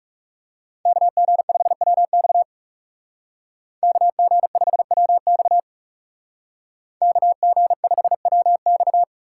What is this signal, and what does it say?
Morse code, heard as a keyed tone, about 37 wpm, 700 Hz. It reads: KG5WX KG5WX KG5WX
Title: Speed Racing - USA 2x2 Call Signs 30wpm